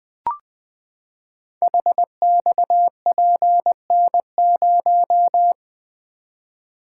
Morse code, heard as a keyed tone, about 20 wpm, 700 Hz. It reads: HXPN0
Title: Code Groups - 5 Letter:Numbers - 20wpm